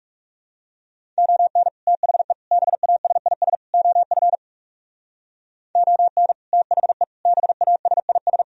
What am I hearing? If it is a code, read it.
Morse code, heard as a keyed tone, about 38 wpm, 690 Hz. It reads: ON THE BASIS OF ON THE BASIS